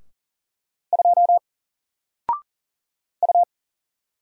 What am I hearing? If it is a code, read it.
Morse code, heard as a keyed tone, about 39 wpm, 700 Hz. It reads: 2 U
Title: Single Letter-Number 40wpm